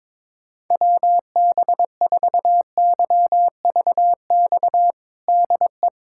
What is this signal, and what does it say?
Morse code, heard as a keyed tone, about 22 wpm, 700 Hz. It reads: WB4YVX DE
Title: USA Callsign DE Callsign - 6 Characters - 22wpm